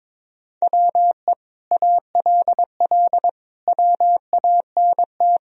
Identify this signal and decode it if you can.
Morse code, heard as a keyed tone, about 22 wpm, 700 Hz. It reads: WE ALL WANT
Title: Sentences from Top 1000 Words - Review Part 4 - 22wpm